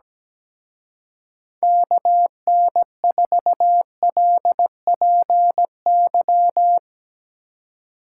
Morse code, heard as a keyed tone, about 18 wpm, 700 Hz. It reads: KN4LPY
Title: Speed Racing - USA Call Signs 17wpm